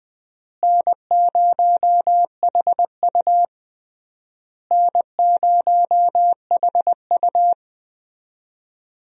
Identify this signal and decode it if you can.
Morse code, heard as a keyed tone, about 21 wpm, 690 Hz. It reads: N0HU N0HU
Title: USA Call Signs - 4 Characters - Repeated 3x - 20wpm